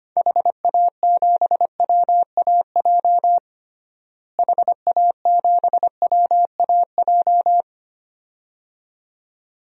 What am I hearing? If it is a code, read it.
Morse code, heard as a keyed tone, about 25 wpm, 700 Hz. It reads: HA7WAJ HA7WAJ
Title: DXCC-100 Call Signs - 6 Characters - Repeated 2x - 25wpm